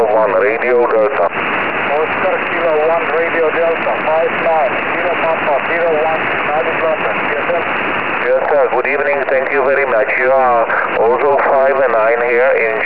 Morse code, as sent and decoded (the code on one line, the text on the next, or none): none